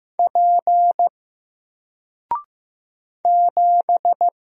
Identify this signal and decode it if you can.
Morse code, heard as a keyed tone, about 15 wpm, 700 Hz. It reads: P 7
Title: Single Letter-Number 15wpm